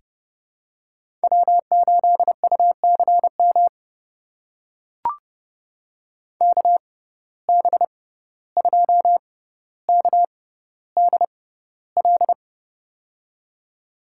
Morse code, heard as a keyed tone, about 32 wpm, spaced as for 15 wpm, 700 Hz. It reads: W8UCM KB2KDL
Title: USA Call Signs - 6x Character Spacing - 30wpm